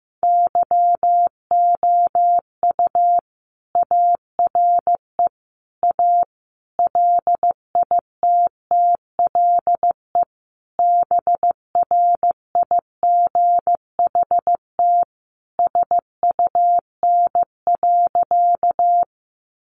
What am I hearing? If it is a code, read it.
Morse code, heard as a keyed tone, about 15 wpm, 700 Hz. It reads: YOU ARE A LITTLE BRIGHT SUN.
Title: Sentences from Top 1000 Words 15wpm